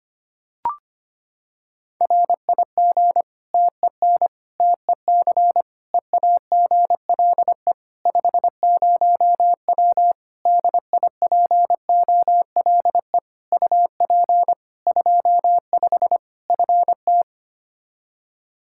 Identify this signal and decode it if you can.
Morse code, heard as a keyed tone, about 25 wpm, 700 Hz. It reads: RIG TEN TEC EAGLE 50W DIPOLE UP 25 FT
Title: Practice QSO - Exchange 2 - Info - 25wpm